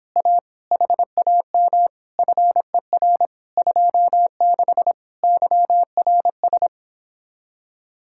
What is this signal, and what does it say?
Morse code, heard as a keyed tone, about 26 wpm, 700 Hz. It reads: A HAM FER 26 YRS